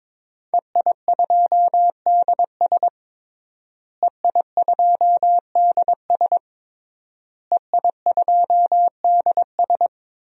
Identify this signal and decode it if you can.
Morse code, heard as a keyed tone, about 21 wpm, 700 Hz. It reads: EI2DS EI2DS EI2DS
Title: DXCC-100 Call Signs - 5 Characters - Repeated 3x - 22wpm